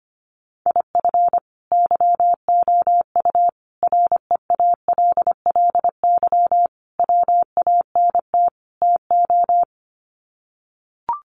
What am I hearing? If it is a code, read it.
Morse code, heard as a keyed tone, about 25 wpm, 700 Hz. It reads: IF YOU REALLY WANT TO
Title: Sets of 5 Words Encore2 25wpm